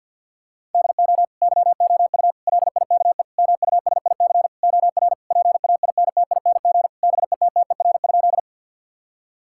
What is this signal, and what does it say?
Morse code, heard as a keyed tone, about 50 wpm, 700 Hz. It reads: DO YOU LIKE MUSIC OR PAINTING BETTER?
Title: Sentences from Top 1000 Words - Review Part 6 - 50wpm